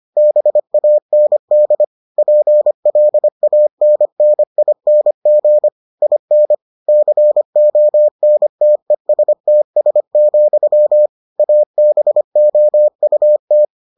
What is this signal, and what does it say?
Morse code, heard as a keyed tone, about 25 wpm, 600 Hz. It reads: BAND PLANNING IN CONTESTS, ABOUT